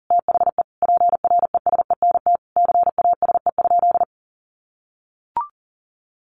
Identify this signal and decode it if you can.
Morse code, heard as a keyed tone, about 40 wpm, 700 Hz. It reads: THE PRESENT CASE?
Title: Sentences from Top 1200 Words 40wpm